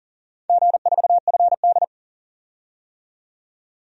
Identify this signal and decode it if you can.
Morse code, heard as a keyed tone, about 39 wpm, 700 Hz. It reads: G4FD
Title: DXCC-100 Call Signs - 4 Characters - 40wpm